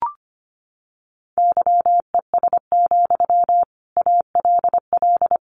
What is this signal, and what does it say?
Morse code, heard as a keyed tone, about 25 wpm, 700 Hz. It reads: YES, ALL
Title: Sentences from Top 500 Words 25wpm